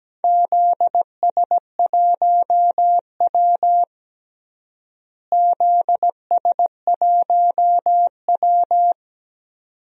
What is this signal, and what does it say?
Morse code, heard as a keyed tone, about 17 wpm, 700 Hz. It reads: ZS1W ZS1W
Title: DXCC-100 Call Signs - 4 Characters - Repeated 2x - 17wpm